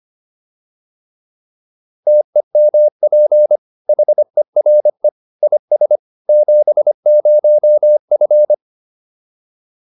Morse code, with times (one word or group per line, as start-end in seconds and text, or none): 2.1-3.6 TEMP
3.9-5.1 HERE
5.4-6.0 IS
6.3-8.5 70F